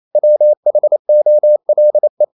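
Morse code, heard as a keyed tone, about 28 wpm, 600 Hz. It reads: WHOLE
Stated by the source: Morse code practice at 28 wpm, text of Ken Follett's novel Never in English, from CW med Gnister.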